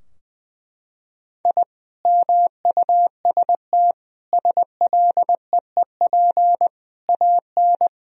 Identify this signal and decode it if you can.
Morse code, heard as a keyed tone, about 21 wpm, 700 Hz. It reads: I MUST SLEEP AN
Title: Sentences from Top 1000 Words - Review Part 4 - 20wpm